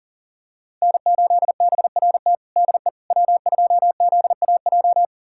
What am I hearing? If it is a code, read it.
Morse code, heard as a keyed tone, about 40 wpm, 700 Hz. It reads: N8BRT DE W2ZAJ
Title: USA Callsign DE Callsign - 5 Characters - 40wpm